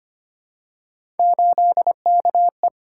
Morse code, heard as a keyed tone, about 25 wpm, 700 Hz. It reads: 8KE
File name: Three Letter-Number Combinations - 25wpm